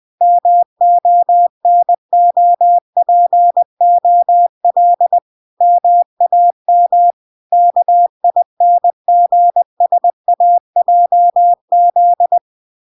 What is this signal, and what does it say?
Morse code, heard as a keyed tone, about 20 wpm, 700 Hz. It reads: MONOPOL MAM KINGSAJZ